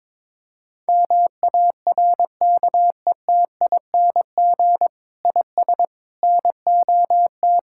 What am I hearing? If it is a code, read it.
Morse code, heard as a keyed tone, about 22 wpm, 700 Hz. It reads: MARKETING IS NOT
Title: Sentences from Top 1000 Words - Review Part 5 - 22wpm